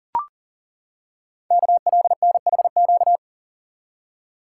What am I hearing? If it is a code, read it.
Morse code, heard as a keyed tone, about 40 wpm, 700 Hz. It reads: KLNHQ
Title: Code Groups - 5 Letters - 40wpm